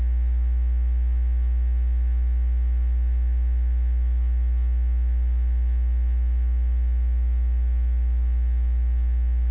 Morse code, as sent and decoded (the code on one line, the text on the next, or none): none